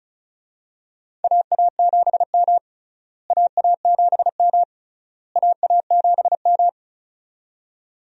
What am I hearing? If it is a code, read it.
Morse code, heard as a keyed tone, about 36 wpm, 700 Hz. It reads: AA7M AA7M AA7M
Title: USA Call Signs - 4 Characters - Repeated 3x - 35wpm